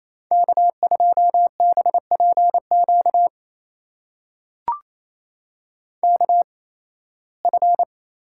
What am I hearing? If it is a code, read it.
Morse code, heard as a keyed tone, about 28 wpm, spaced as for 18 wpm, 700 Hz. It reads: K2BPQ KF
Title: USA Call Signs - 5 Characters - 8x Character Spacing - 28wpm